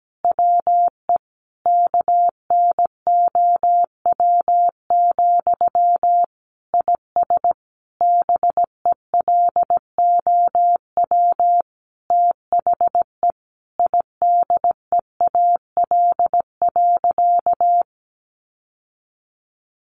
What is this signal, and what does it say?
Morse code, heard as a keyed tone, about 17 wpm, 700 Hz. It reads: WE KNOW, IS BELOW THE IDEAL.
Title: Sentences from Top 1400 Words 17wpm